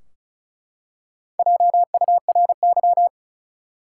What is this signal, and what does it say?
Morse code, heard as a keyed tone, about 36 wpm, 700 Hz. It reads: JURY
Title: Four Letter Words - 35wpm